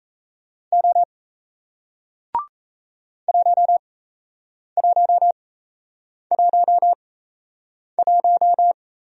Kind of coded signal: Morse code

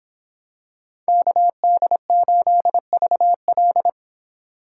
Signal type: Morse code

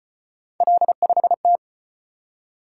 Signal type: Morse code

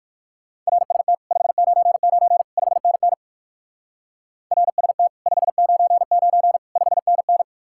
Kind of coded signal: Morse code

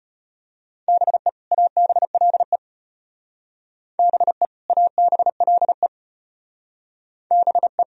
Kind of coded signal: Morse code

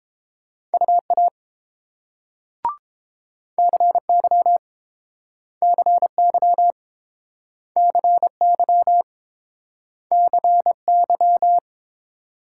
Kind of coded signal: Morse code

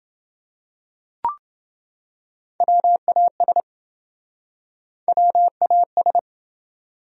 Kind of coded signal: Morse code

